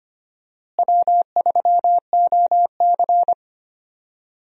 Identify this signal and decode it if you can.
Morse code, heard as a keyed tone, about 25 wpm, 700 Hz. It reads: W3OC
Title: USA Call Signs - 4 Characters - Repeated 2x - 25wpm